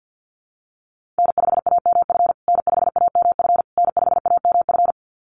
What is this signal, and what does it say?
Morse code, heard as a keyed tone, about 49 wpm, 700 Hz. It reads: N5AMF N5AMF N5AMF
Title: USA Call Signs Repeated 3x 50wpm